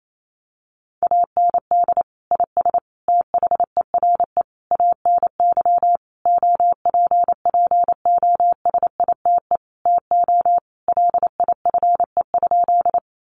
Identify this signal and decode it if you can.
Morse code, heard as a keyed tone, about 28 wpm, 700 Hz. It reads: AND IS THERE ANY OPPOSITE TO LIFE?
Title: Sentences from Top 1300 Words 28wpm